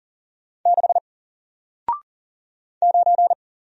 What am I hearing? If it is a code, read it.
Morse code, heard as a keyed tone, about 41 wpm, 700 Hz. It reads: 6 9
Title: Single Letter-Number Alt 40wpm